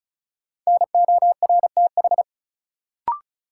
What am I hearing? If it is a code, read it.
Morse code, heard as a keyed tone, about 36 wpm, 700 Hz. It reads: NORTH
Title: Sets of 5 Words Encore - New Word Drill - 35wpm